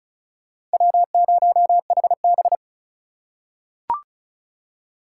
Morse code, heard as a keyed tone, about 36 wpm, 700 Hz. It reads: W0HB
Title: USA Call Signs - 4 Characters - 3x Character Spacing - 35wpm